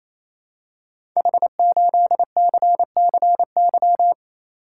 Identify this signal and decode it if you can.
Morse code, heard as a keyed tone, about 27 wpm, 700 Hz. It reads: H8CCY
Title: Code Groups - 5 Letter:Numbers - 28wpm